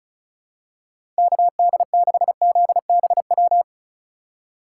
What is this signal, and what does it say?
Morse code, heard as a keyed tone, about 34 wpm, 700 Hz. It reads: KD6ZBW
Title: USA Call Signs - 6 Characters - 3x Character Spacing - 35wpm